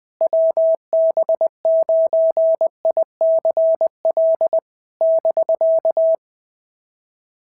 Morse code, heard as a keyed tone, about 20 wpm, 650 Hz. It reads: WB9ICL <BK>
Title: SOTA QSO Practice 20wpm